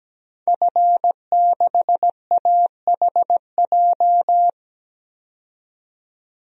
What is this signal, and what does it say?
Morse code, heard as a keyed tone, about 17 wpm, 700 Hz. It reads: F6AHJ
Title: DXCC-100 Call Signs - 5 Characters - 17wpm